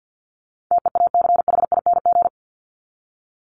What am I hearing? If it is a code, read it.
Morse code, heard as a keyed tone, about 50 wpm, 700 Hz. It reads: TEACHING
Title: ING Suffix - Top 100 Words with Base Words 50wpm